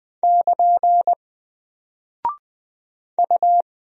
Morse code, heard as a keyed tone, about 20 wpm, 700 Hz. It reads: <KN> U
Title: Single Character (Letter:Number:Punctuation:Prosign) Alt 20wpm